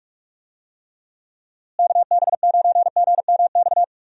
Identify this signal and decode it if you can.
Morse code, heard as a keyed tone, about 42 wpm, 700 Hz. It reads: KB9GMX
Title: USA Call Signs - 6 Characters - Repeated 3x - 45wpm